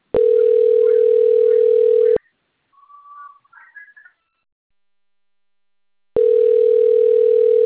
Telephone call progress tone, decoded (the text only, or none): ringback tone